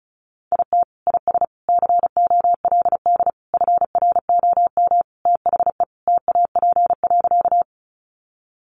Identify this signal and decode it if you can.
Morse code, heard as a keyed tone, about 34 wpm, 700 Hz. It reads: IT IS COLD FROM THE TAP.
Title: Sentences from Top 1400 Words 35wpm